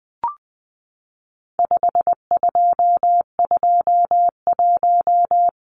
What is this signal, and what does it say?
Morse code, heard as a keyed tone, about 20 wpm, 700 Hz. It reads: 5221
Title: Numbers - 4-Digit 20wpm